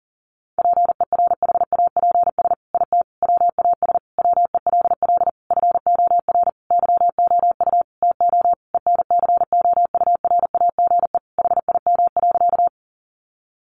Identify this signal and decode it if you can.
Morse code, heard as a keyed tone, about 37 wpm, 700 Hz. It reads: PERHAPS IT WAS WELL FOR YOU TO ENCOURAGE HIM.
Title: Sentences from Top 1400 Words 40wpm